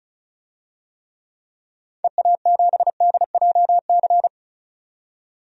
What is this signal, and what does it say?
Morse code, heard as a keyed tone, about 36 wpm, 700 Hz. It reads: EA7DJC
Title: DXCC-100 Call Signs - 6 Characters - 35wpm